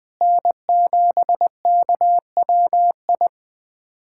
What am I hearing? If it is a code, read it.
Morse code, heard as a keyed tone, about 20 wpm, 700 Hz. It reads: N7KWI
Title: USA Callsign DE Callsign 20wpm